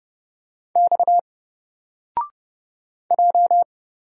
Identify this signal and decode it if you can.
Morse code, heard as a keyed tone, about 30 wpm, 700 Hz. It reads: X J
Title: Single Character (Letter:Number:Punctuation:Prosign) 30wpm